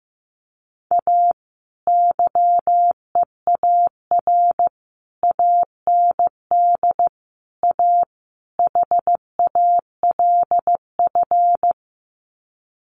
Morse code, heard as a keyed tone, about 15 wpm, 700 Hz. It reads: A YEAR AND A HALF
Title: Sets of 5 Words 15wpm